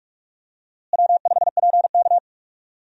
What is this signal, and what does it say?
Morse code, heard as a keyed tone, about 44 wpm, 700 Hz. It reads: W5PK
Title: USA Call Signs - 4 Characters - Repeated 2x - 45wpm